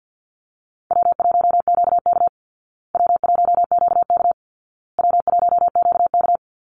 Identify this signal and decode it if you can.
Morse code, heard as a keyed tone, about 50 wpm, 700 Hz. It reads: W1QK W1QK W1QK
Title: Easy Call Signs Repeated 3x - 50wpm